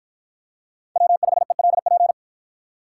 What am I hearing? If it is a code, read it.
Morse code, heard as a keyed tone, about 53 wpm, 700 Hz. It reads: W5ELP